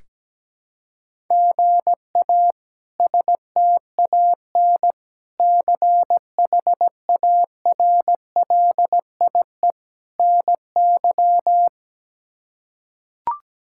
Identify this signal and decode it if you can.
Morse code, heard as a keyed tone, about 17 wpm, 700 Hz. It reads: GA STAN CHARLIE NY
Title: K1USN SST QSO Practice - 17wpm